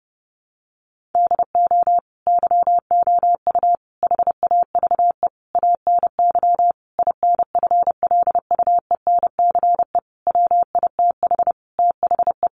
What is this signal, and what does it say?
Morse code, heard as a keyed tone, about 30 wpm, 700 Hz. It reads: DO YOU HAVE ANY INFLUENCE WITH THE
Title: Sentences from Top 1100 Words 30wpm